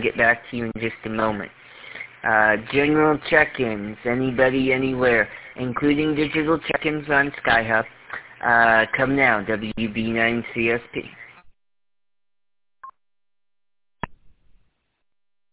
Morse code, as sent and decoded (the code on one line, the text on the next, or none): none